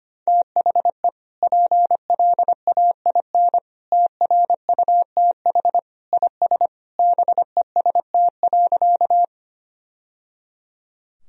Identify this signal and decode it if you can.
Morse code, heard as a keyed tone, about 25 wpm, 700 Hz. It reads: THE PLAIN TRUTH IS BEST.
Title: Sentences from Top 1000 Words - Review Part 2 - 25wpm